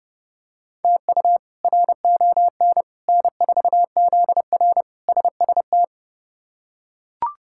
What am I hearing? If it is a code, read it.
Morse code, heard as a keyed tone, about 32 wpm, 700 Hz. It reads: TU RON N4ZR SST
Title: K1USN SST QSO Practice - 30wpm